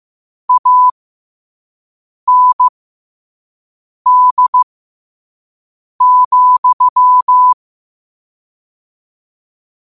Morse code, heard as a keyed tone, about 15 wpm, spaced as for 5 wpm, 1000 Hz. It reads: AND,
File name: qotdfiles / 5wpm